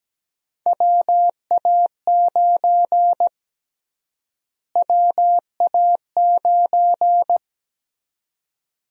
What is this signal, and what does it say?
Morse code, heard as a keyed tone, about 17 wpm, 700 Hz. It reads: WA9 WA9
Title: USA Call Sign Prefixes - Repeated 2x - 17wpm